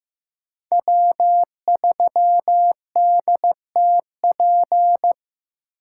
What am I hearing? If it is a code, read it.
Morse code, heard as a keyed tone, about 15 wpm, 700 Hz. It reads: W3DTP
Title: USA Call Signs - 5 Characters - 2x Character Spacing - 15wpm